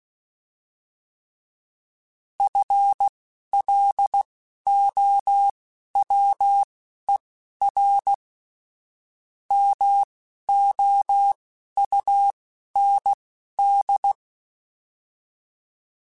Morse code, heard as a keyed tone, about 16 wpm, spaced as for 11 wpm, 800 Hz. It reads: FLOWER MOUND